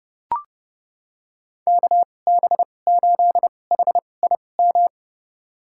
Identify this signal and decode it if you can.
Morse code, heard as a keyed tone, about 31 wpm, spaced as for 22 wpm, 700 Hz. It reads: KB8HIM
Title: USA Call Signs - 2x Character Spacing - 30wpm